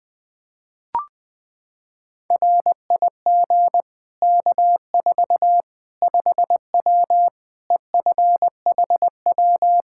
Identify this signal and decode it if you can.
Morse code, heard as a keyed tone, about 20 wpm, 700 Hz. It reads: RIG K4 5W EFHW
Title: Practice QSO - Exchange 2 - Info - 20wpm